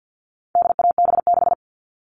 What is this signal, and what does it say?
Morse code, heard as a keyed tone, about 50 wpm, 700 Hz. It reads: DAB6
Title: All Cut Numbers - 4-Digit 50wpm